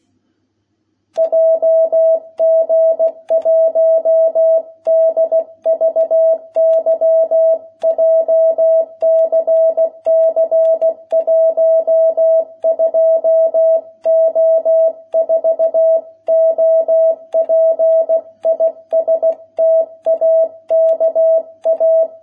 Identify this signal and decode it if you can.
Morse code, heard as a keyed tone, about 16 wpm, 650 Hz. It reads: JG1DVY JCC 12O4 OP IS TAKA